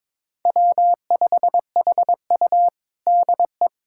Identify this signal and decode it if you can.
Morse code, heard as a keyed tone, about 22 wpm, 700 Hz. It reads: W5HU DE
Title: USA Callsign DE Callsign - 4 Characters - 22wpm